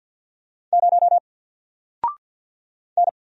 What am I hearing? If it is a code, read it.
Morse code, heard as a keyed tone, about 50 wpm, 700 Hz. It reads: 0 N
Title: Single Letter-Number 50wpm